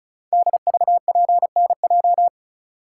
Morse code, heard as a keyed tone, about 36 wpm, 700 Hz. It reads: DVPNJ
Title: Code Groups - 5 Letters - 35wpm